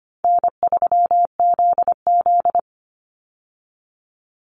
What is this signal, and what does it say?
Morse code, heard as a keyed tone, about 25 wpm, 700 Hz. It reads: N3ZZ
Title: Easy Call Signs Repeated 2x - 25wpm